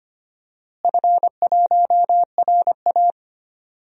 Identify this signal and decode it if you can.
Morse code, heard as a keyed tone, about 25 wpm, 700 Hz. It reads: F1RA
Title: DXCC-100 Call Signs - 4 Characters - Repeated 3x - 25wpm